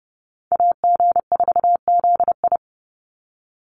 Morse code, heard as a keyed tone, about 30 wpm, 700 Hz. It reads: AG4ZI
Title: USA Call Signs 30wpm